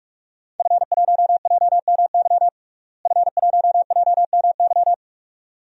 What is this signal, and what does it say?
Morse code, heard as a keyed tone, about 43 wpm, 700 Hz. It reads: F1JMY F1JMY
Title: DXCC-100 Call Signs - 5 Characters - Repeated 2x - 45wpm